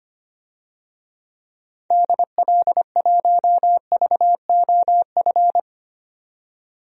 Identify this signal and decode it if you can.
Morse code, heard as a keyed tone, about 25 wpm, 700 Hz. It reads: DL1VOF
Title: DXCC-100 Call Signs - 6 Characters - 25wpm